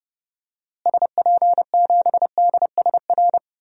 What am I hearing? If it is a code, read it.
Morse code, heard as a keyed tone, about 30 wpm, 700 Hz. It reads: SP7DSR
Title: DXCC-100 Call Signs - 6 Characters - Repeated 3x - 30wpm